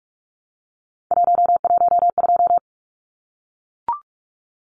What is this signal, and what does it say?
Morse code, heard as a keyed tone, about 44 wpm, 700 Hz. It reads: 112
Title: Numbers - 3-Digit 45wpm